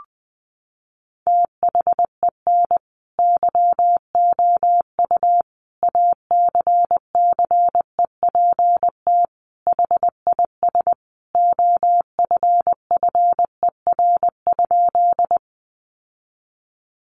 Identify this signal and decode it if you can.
Morse code, heard as a keyed tone, about 20 wpm, 700 Hz. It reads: THEN YOU ACCEPT HIS OFFER?
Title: Sentences from Top 1300 Words 20wpm